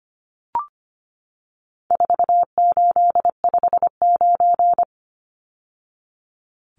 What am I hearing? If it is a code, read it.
Morse code, heard as a keyed tone, about 25 wpm, 700 Hz. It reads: 4859
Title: Numbers - 4-Digit 25wpm